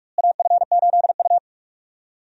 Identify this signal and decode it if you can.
Morse code, heard as a keyed tone, about 46 wpm, 700 Hz. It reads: AF8U
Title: USA Call Signs - 4 Characters - Repeated 2x - 45wpm